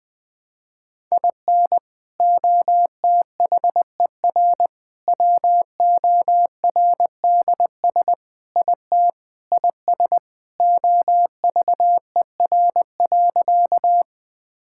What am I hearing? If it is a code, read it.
Morse code, heard as a keyed tone, about 20 wpm, 700 Hz. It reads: IN OTHER WORDS IT IS OVER.
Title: Sentences with 3-Word Phrases - 20wpm